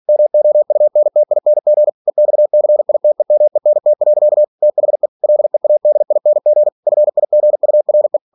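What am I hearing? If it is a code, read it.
Morse code, heard as a keyed tone, about 46 wpm, 600 Hz. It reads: MOUNTING EXCITEMENT. THE LEADING FIGURE
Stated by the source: Practice text from an English novel sent at 47 WPM.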